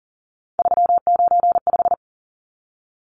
Morse code, heard as a keyed tone, about 40 wpm, 700 Hz. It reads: 395
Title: Numbers - 3-Digit 40wpm